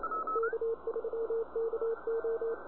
Morse code, heard as a keyed tone, about 28 wpm, 410 Hz, with radio noise. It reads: K3KO